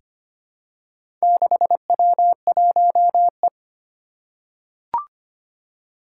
Morse code, heard as a keyed tone, about 25 wpm, 700 Hz. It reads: 6W1E
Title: Four Challenging Letter-Number Combinations 25wpm